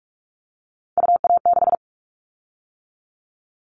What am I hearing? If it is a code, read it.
Morse code, heard as a keyed tone, about 44 wpm, 700 Hz. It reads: UA6
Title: Callsign Prefixes for Top 100 DXCC - 45wpm